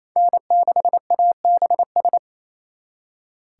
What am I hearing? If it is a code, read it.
Morse code, heard as a keyed tone, about 29 wpm, 700 Hz. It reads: N6ABS
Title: Speed Racing - USA 1x3 Call Signs 28wpm